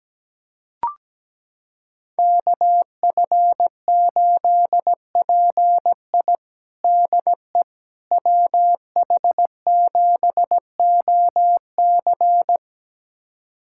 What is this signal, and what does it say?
Morse code, heard as a keyed tone, about 17 wpm, 700 Hz. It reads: KF8PI DE WH7OC
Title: USA Callsign DE Callsign - 5 Characters - 17wpm